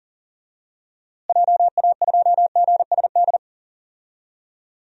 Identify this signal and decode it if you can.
Morse code, heard as a keyed tone, about 37 wpm, 700 Hz. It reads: JA2GSD
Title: DXCC-100 Call Signs - 6 Characters - Repeated 2x - 40wpm